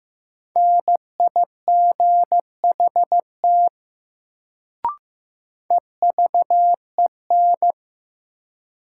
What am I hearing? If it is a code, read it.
Morse code, heard as a keyed tone, about 15 wpm, 700 Hz. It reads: NIGHT EVEN
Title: Sets of 4 Words - New Word Drill - 15wpm